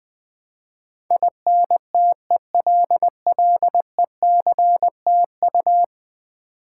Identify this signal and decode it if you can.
Morse code, heard as a keyed tone, about 20 wpm, 700 Hz. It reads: INTELLECTU
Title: Build A Word - 5 Letters At A Time - 12-Letter Words - 20wpm